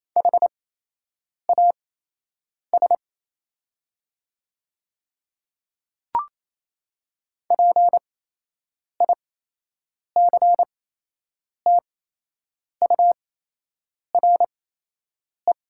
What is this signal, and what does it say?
Morse code, heard as a keyed tone, about 28 wpm, spaced as for 8 wpm, 700 Hz. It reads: HAS PICTURE
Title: 8x Character Spacing - Top 300 Words - 28wpm